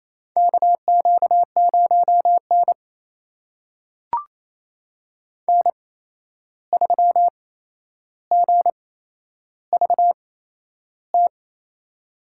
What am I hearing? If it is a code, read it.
Morse code, heard as a keyed tone, about 28 wpm, spaced as for 11 wpm, 700 Hz. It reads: KQ0N N3GVT
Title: USA Call Signs - 8x Character Spacing - 28wpm